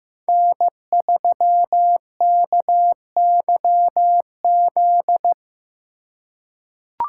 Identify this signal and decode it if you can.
Morse code, heard as a keyed tone, about 15 wpm, 700 Hz. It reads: N3KYZ
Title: USA Call Signs - 2x Character Spacing - 15wpm